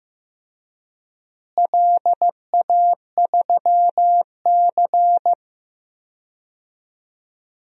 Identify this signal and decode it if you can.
Morse code, heard as a keyed tone, about 15 wpm, 700 Hz. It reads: LA3C
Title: DXCC-100 Call Signs - 4 Characters - Repeated 2x - 15wpm